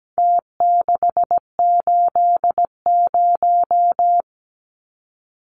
Morse code, heard as a keyed tone, about 17 wpm, 700 Hz. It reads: T680
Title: Numbers - Contest Style - 4-Digit 17wpm